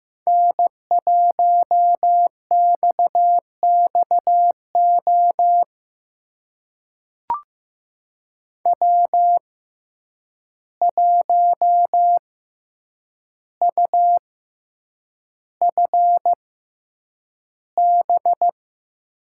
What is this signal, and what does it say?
Morse code, heard as a keyed tone, about 15 wpm, spaced as for 8 wpm, 700 Hz. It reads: N1XXO W1UFB
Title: USA Call Signs - 5 Characters - 6x Character Spacing - 15wpm